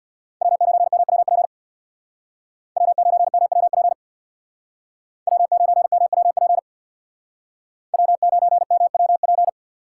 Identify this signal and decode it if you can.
Morse code, heard as a keyed tone, about 64 wpm, 700 Hz. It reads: W9MWP W9MWP W9MWP W9MWP